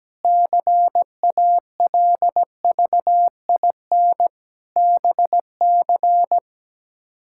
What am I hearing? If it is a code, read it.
Morse code, heard as a keyed tone, about 17 wpm, 700 Hz. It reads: CALVIN BC
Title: K1USN SST QSO Practice - 17wpm